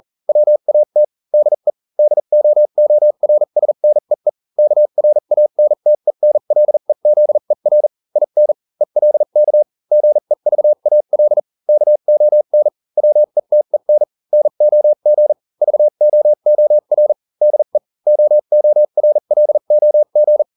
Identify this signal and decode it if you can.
Morse code, heard as a keyed tone, about 40 wpm, spaced as for 35 wpm, 600 Hz. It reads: WAT DE DOORSNEE KRANTENLEZER IN ELK GEVAL KON WETEN NOG VOOR DE OORLOG